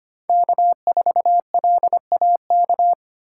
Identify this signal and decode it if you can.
Morse code, heard as a keyed tone, about 25 wpm, 700 Hz. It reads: K4LAK
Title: USA Call Signs - 5 Characters - 4x Character Spacing - 25wpm